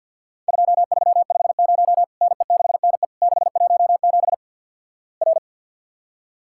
Morse code, heard as a keyed tone, about 50 wpm, 700 Hz. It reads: 2350 NE6NE 617 R
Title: Worked All Europe DX Contest - QTC Exchange - 50wpm